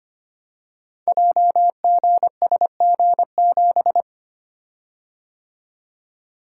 Morse code, heard as a keyed tone, about 25 wpm, 700 Hz. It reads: JGSG7
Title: Code Groups - 5 Letter:Numbers - 25wpm